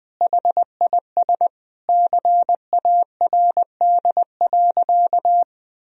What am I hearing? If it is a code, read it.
Morse code, heard as a keyed tone, about 20 wpm, 700 Hz. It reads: HIS CARD.
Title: Sentences from Top 1000 Words - Review Part 4 - 20wpm